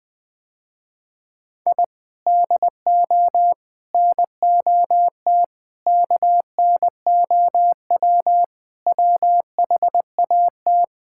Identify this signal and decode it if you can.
Morse code, heard as a keyed tone, about 21 wpm, 700 Hz. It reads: I DO NOT KNOW WHAT
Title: Sentences with 5-Word Phrases - Spoken Beginning - 20wpm